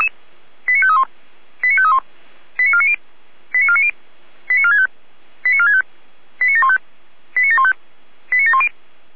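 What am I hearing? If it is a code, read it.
Five-tone selective call (ZVEI), heard as quick runs of five tones, tones of about 70 ms. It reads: 89531 89531 89490 89490 89465 89465 89715 89715 89710
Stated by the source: SDR-received ZVEI five-tone call 89518.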